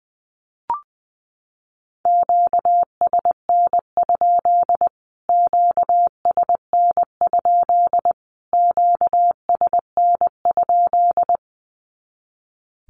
Morse code, heard as a keyed tone, about 20 wpm, 700 Hz. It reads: QSN? QSN? QSN?
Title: Q-Codes Repeated 3x 20wpm